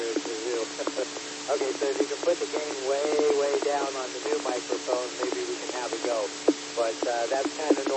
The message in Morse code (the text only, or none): none